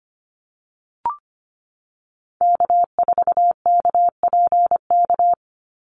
Morse code, heard as a keyed tone, about 25 wpm, 700 Hz. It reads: K4KPK
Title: NA SOTA Activator Callsigns 25wpm